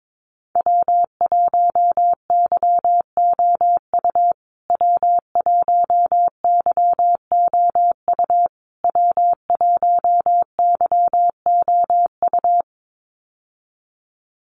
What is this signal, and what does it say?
Morse code, heard as a keyed tone, about 22 wpm, 700 Hz. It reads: W1YOU W1YOU W1YOU
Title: USA Call Signs Repeated 3x 22wpm